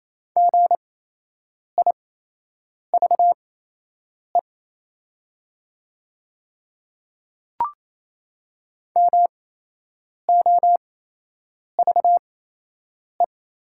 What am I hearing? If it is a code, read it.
Morse code, heard as a keyed tone, about 28 wpm, spaced as for 8 wpm, 700 Hz. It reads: GIVE MOVE
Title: 8x Character Spacing - Top 200 Words - 28wpm